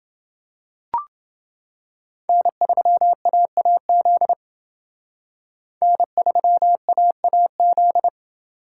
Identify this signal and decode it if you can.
Morse code, heard as a keyed tone, about 29 wpm, 700 Hz. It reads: N3AAZ N3AAZ